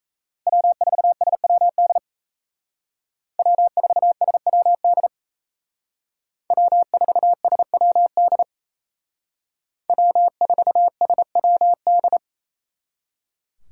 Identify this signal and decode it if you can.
Morse code, heard as a keyed tone, about 36 wpm, 700 Hz. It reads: W4SWD W4SWD W4SWD W4SWD